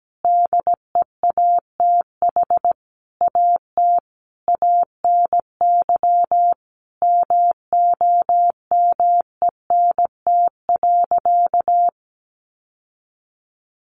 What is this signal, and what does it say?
Morse code, heard as a keyed tone, about 17 wpm, 700 Hz. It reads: DEATH AT ANY MOMENT.
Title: Sentences from Top 1100 Words 17wpm